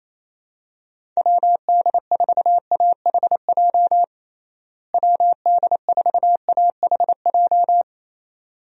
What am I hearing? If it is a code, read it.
Morse code, heard as a keyed tone, about 28 wpm, 700 Hz. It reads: WD4AHJ WD4AHJ
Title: USA Call Signs - 6 Characters - Repeated 2x - 28wpm